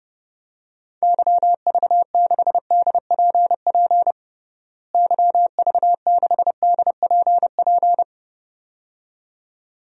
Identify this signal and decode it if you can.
Morse code, heard as a keyed tone, about 30 wpm, 700 Hz. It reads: YV6DPP YV6DPP
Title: DXCC-100 Call Signs - 6 Characters - Repeated 2x - 30wpm